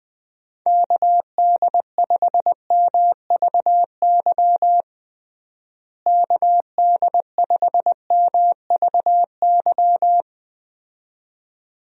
Morse code, heard as a keyed tone, about 20 wpm, 700 Hz. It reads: KD5MVY KD5MVY
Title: USA Call Signs - 6 Characters - Repeated 2x - 20wpm